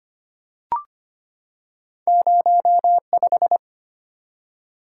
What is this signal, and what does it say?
Morse code, heard as a keyed tone, about 25 wpm, 700 Hz. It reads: 05
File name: Two Letter-Number Combinations - 25wpm